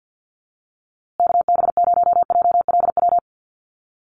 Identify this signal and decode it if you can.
Morse code, heard as a keyed tone, about 50 wpm, 700 Hz. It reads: KB0JLW
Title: USA Call Signs Repeated 2x 50wpm